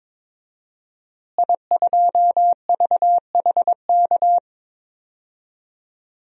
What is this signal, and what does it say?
Morse code, heard as a keyed tone, about 22 wpm, 700 Hz. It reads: I2VHK
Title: DXCC-100 Call Signs - 5 Characters - Repeated 3x - 22wpm